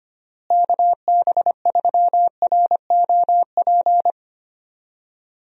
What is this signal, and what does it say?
Morse code, heard as a keyed tone, about 25 wpm, 700 Hz. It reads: KB3ROP
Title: USA Call Signs - 6 Characters - Repeated 3x - 25wpm